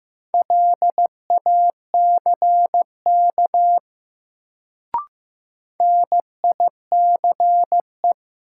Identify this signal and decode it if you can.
Morse code, heard as a keyed tone, about 15 wpm, 700 Hz. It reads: LACK NICE
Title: Sentences from Top 1100 Words - New Word Drill Alt - 15wpm